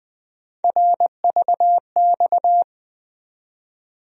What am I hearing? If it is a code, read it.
Morse code, heard as a keyed tone, about 20 wpm, 700 Hz. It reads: RVX
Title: USA Call Sign Postfixes - 3 Letters 20wpm